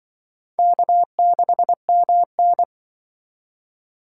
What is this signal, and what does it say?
Morse code, heard as a keyed tone, about 24 wpm, 700 Hz. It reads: K6MN
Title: Speed Racing - USA 1x2 Call Signs 20wpm